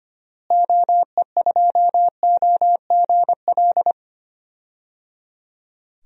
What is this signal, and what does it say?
Morse code, heard as a keyed tone, about 25 wpm, 700 Hz. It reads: OE2OGL
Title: DXCC-100 Call Signs - 6 Characters - Repeated 3x - 25wpm